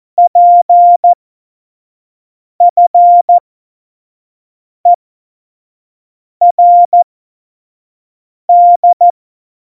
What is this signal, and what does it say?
Morse code, heard as a keyed tone, about 14 wpm, spaced as for 5 wpm, 700 Hz. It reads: PFERD
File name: Morsekurs / Witze / Witz 07 Juli 2020, 25 F70